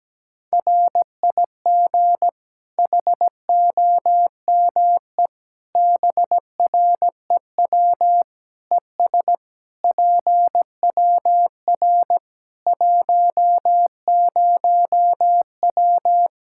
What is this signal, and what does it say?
Morse code, heard as a keyed tone, about 17 wpm, 700 Hz. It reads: RIG HOME BREW ES PWR 10W
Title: Practice QSO - Exchange 2 - Info - 17wpm